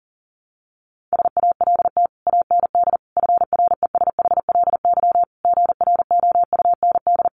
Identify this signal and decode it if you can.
Morse code, heard as a keyed tone, about 40 wpm, 700 Hz. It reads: SALT AND FRESHLY GROUND
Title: Sets of 5 Words Encore 40wpm